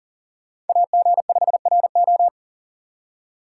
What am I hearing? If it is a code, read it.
Morse code, heard as a keyed tone, about 52 wpm, 700 Hz. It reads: AG5RO